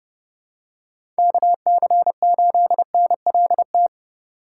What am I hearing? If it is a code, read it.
Morse code, heard as a keyed tone, about 30 wpm, 700 Hz. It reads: KC8NLT
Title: USA Call Signs - 6 Characters - 2x Character Spacing - 30wpm